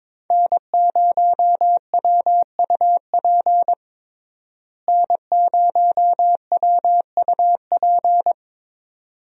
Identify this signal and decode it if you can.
Morse code, heard as a keyed tone, about 22 wpm, 700 Hz. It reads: N0WUP N0WUP
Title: USA Call Signs - 5 Characters - Repeated 2x - 22wpm